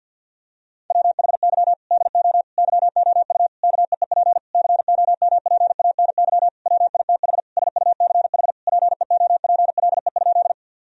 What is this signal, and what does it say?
Morse code, heard as a keyed tone, about 49 wpm, 700 Hz. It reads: WHY DO YOU KEEP COMPANY WITH SUCH PEOPLE?
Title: Sentences from Top 1000 Words - Review Part 3 - 50wpm